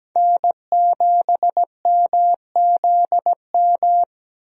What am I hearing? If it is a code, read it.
Morse code, heard as a keyed tone, about 17 wpm, 700 Hz. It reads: N7MZM
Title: USA Call Signs - 5 Characters - Repeated 3x - 17wpm